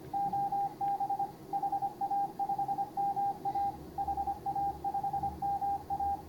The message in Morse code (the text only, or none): OB HA5MA HA5MA